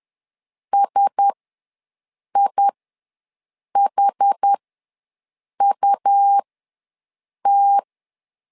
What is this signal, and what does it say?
Morse code, heard as a keyed tone, about 10 wpm, spaced as for 6 wpm, 800 Hz. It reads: SIHUT
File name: LEZIONI / 4 Q1S ws 10wpm 2208